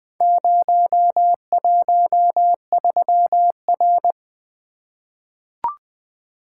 Morse code, heard as a keyed tone, about 20 wpm, 700 Hz. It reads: 013R
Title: Four Letter-Number Combinations - 20wpm